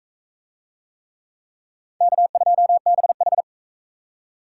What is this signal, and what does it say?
Morse code, heard as a keyed tone, about 38 wpm, 700 Hz. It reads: K2BH